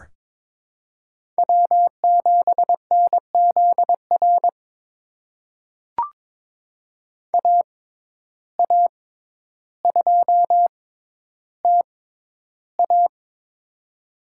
Morse code, heard as a keyed tone, about 22 wpm, spaced as for 11 wpm, 700 Hz. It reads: W7NZR AA2TA
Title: USA Call Signs - 5 Characters - 6x Character Spacing - 22wpm